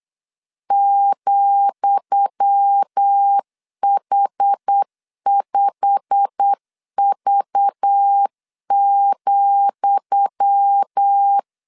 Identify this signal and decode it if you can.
Morse code, heard as a keyed tone, about 8 wpm, 800 Hz. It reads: ,H5V,
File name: LEZIONI / 10 SIGNS / 10D COMA 8wpm 3749